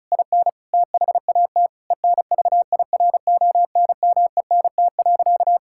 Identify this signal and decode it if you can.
Morse code, heard as a keyed tone, about 34 wpm, 700 Hz. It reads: IN THAT ENVIRONMENT.
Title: Sentences from Top 1000 Words - Review Part 5 - 35wpm